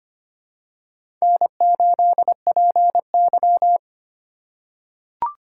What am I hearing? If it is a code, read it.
Morse code, heard as a keyed tone, about 25 wpm, 700 Hz. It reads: N8PY
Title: USA Call Signs - 4 Characters - 2x Character Spacing - 25wpm